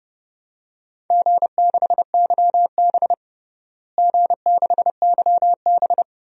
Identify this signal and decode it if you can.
Morse code, heard as a keyed tone, about 30 wpm, 700 Hz. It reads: G6YB G6YB
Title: DXCC-100 Call Signs - 4 Characters - Repeated 3x - 30wpm